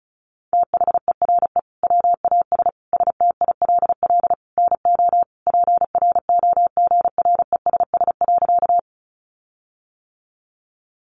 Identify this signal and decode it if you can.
Morse code, heard as a keyed tone, about 34 wpm, 700 Hz. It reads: THERE WAS STILL NO PROGRESS.
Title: Sentences from Top 1400 Words 35wpm